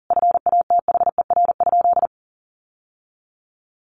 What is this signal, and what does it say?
Morse code, heard as a keyed tone, about 37 wpm, 700 Hz. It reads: FATHER?
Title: Sentences from Top 700 Words 40wpm